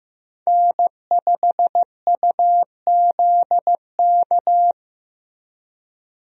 Morse code, heard as a keyed tone, about 15 wpm, 700 Hz. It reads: N5UZK
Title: USA Call Signs - 5 Characters - Repeated 2x - 15wpm